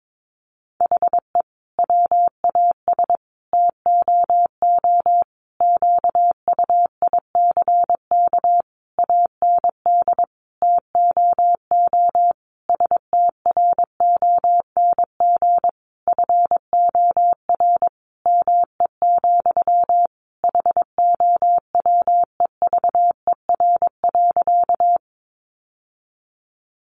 Morse code, heard as a keyed tone, about 22 wpm, 700 Hz. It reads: HE WAS TOO QUICK AND TOO STRONG FOR ME, HOWEVER.
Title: Sentences from Top 600 Words 22wpm